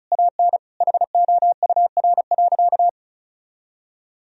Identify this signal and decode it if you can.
Morse code, heard as a keyed tone, about 36 wpm, 700 Hz. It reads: AN HOUR.
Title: Sentences from Top 1000 Words - Review Part 1 - 35wpm